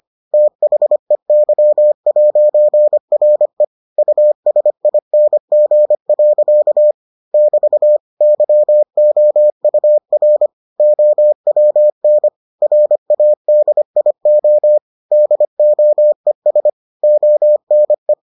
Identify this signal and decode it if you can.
Morse code, heard as a keyed tone, about 25 wpm, 600 Hz. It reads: THEY'RE USING. <BT> YOUR OWN RADIO DOES ONE